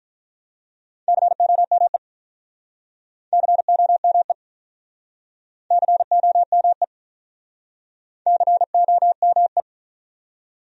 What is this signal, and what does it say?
Morse code, heard as a keyed tone, about 48 wpm, 700 Hz. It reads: COME COME COME COME